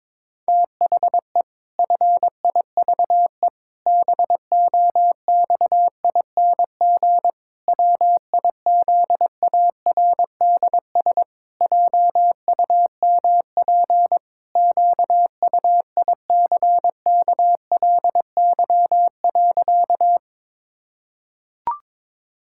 Morse code, heard as a keyed tone, about 22 wpm, 700 Hz. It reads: THE FIVE BOXING WIZARDS JUMP QUICKLY.
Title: Pangram Challenge v2 - 22wpm